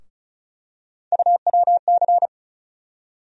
Morse code, heard as a keyed tone, about 34 wpm, 700 Hz. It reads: UWC
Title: USA Call Sign Postfixes - 3 Letters 35wpm